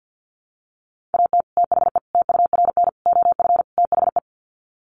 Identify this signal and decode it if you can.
Morse code, heard as a keyed tone, about 50 wpm, 700 Hz. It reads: AT THE TURN OF THE
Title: Sets of 5 Words 50wpm